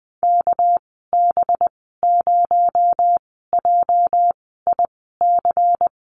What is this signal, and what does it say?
Morse code, heard as a keyed tone, about 20 wpm, spaced as for 15 wpm, 700 Hz. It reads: KB0JIC
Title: 245 Lesson 39g - Letter z - Call Signs 20wpm